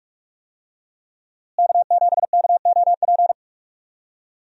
Morse code, heard as a keyed tone, about 41 wpm, 700 Hz. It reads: K7KOP
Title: USA Call Signs - 5 Characters - Repeated 2x - 45wpm